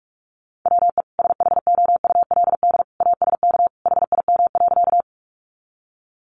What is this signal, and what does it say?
Morse code, heard as a keyed tone, about 44 wpm, 700 Hz. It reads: WE SHOULD ASK HIM.
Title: Sentences from Top 300 Words 45wpm